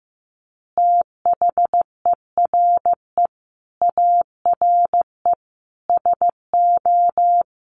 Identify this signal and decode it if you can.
Morse code, heard as a keyed tone, about 15 wpm, 700 Hz. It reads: THERE ARE SO
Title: Additional Easy Phrases 15wpm